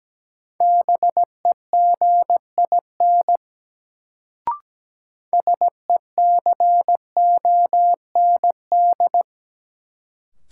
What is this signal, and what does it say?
Morse code, heard as a keyed tone, about 17 wpm, 700 Hz. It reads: BEGIN SECOND
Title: Sentences from Top 300 Words - New Word Drill - 17wpm